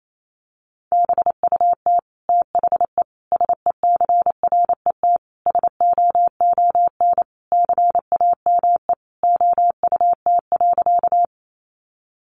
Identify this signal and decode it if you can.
Morse code, heard as a keyed tone, about 28 wpm, 700 Hz. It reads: BUT THE SECRET SOON CAME OUT.
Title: Sentences from Top 1200 Words 28wpm